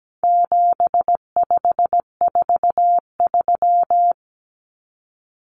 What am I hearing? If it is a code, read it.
Morse code, heard as a keyed tone, about 17 wpm, 700 Hz. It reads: 7543
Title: Common Cut Numbers - 4-Digit 17wpm